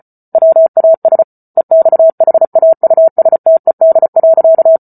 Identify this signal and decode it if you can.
Morse code, heard as a keyed tone, about 34 wpm, 650 Hz. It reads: WAS EXHAUSTED.